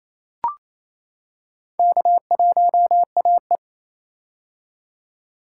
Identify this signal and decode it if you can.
Morse code, heard as a keyed tone, about 28 wpm, 700 Hz. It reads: K1AE
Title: K1USN SST QSO Practice - 28wpm